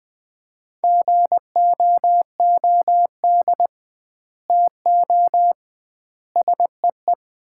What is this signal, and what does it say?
Morse code, heard as a keyed tone, about 20 wpm, 700 Hz. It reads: GOOD TO SEE
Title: 2x Word-Spacing - Sets of 4 Words - 20wpm